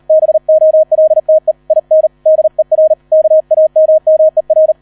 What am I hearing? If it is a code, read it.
Morse code, heard as a keyed tone, about 38 wpm, 630 Hz, with radio noise. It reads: BOLTE IN DER KAMMER
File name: 001 Erster Streich 40wpm